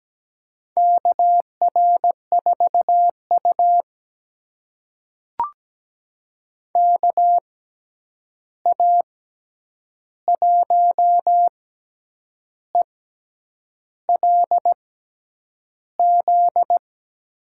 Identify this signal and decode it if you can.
Morse code, heard as a keyed tone, about 17 wpm, spaced as for 8 wpm, 700 Hz. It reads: KR4U KA1ELZ
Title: USA Call Signs - 6x Character Spacing - 17wpm